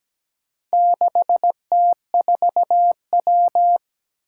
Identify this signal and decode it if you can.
Morse code, heard as a keyed tone, about 17 wpm, 700 Hz. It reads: 6T4W
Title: Four Challenging Letter-Number Combinations 17wpm